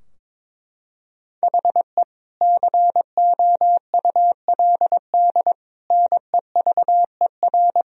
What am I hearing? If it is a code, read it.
Morse code, heard as a keyed tone, about 22 wpm, 700 Hz. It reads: HE COULD NEVER
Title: Sentences from Top 1000 Words - Review Part 3 - 22wpm